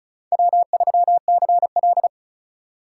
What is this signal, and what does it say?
Morse code, heard as a keyed tone, about 36 wpm, 700 Hz. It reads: W3CL
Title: USA Call Signs - 4 Characters - 2x Character Spacing - 35wpm